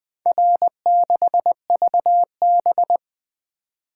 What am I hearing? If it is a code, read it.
Morse code, heard as a keyed tone, about 20 wpm, 700 Hz. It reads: R6VB
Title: Four Letter-Number Combinations - 20wpm